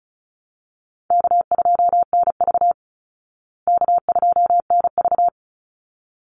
Morse code, heard as a keyed tone, about 34 wpm, 700 Hz. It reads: K2NV K2NV
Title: Call Signs Repeated 3x - 35wpm